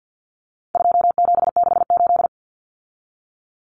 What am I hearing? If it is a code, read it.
Morse code, heard as a keyed tone, about 50 wpm, 700 Hz. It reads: 2768
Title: Common Cut Numbers - 4-Digit 50wpm